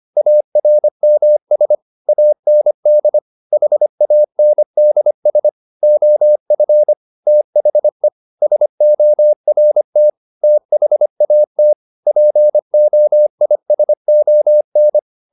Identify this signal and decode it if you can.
Morse code, heard as a keyed tone, about 25 wpm, 600 Hz. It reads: ARMS AND HANDS OF THE SORT THAT POISON
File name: the old man and the sea part 1 25wpm